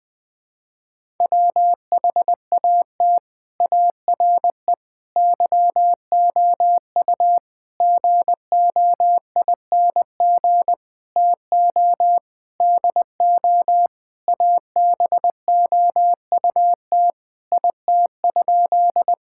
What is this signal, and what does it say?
Morse code, heard as a keyed tone, about 21 wpm, 700 Hz. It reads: WHAT ARE YOU GOING TO DO ABOUT IT?
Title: Sentences with 6-Word Phrases - Spoken Beginning - 20wpm